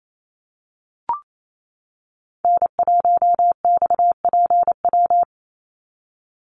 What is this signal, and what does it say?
Morse code, heard as a keyed tone, about 28 wpm, 700 Hz. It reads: N1XPW
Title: USA Call Signs 28wpm